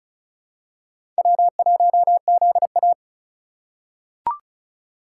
Morse code, heard as a keyed tone, about 36 wpm, 700 Hz. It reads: W1ZA
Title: USA Call Signs - 3x Character Spacing - 35wpm